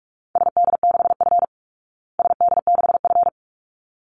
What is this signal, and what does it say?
Morse code, heard as a keyed tone, about 44 wpm, 700 Hz. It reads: SD6F SD6F
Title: Call Signs Repeated 3x - 45wpm